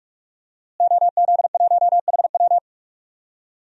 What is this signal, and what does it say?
Morse code, heard as a keyed tone, about 46 wpm, 700 Hz. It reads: OZ1HW
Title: DXCC-100 Call Signs - 5 Characters - Repeated 3x - 45wpm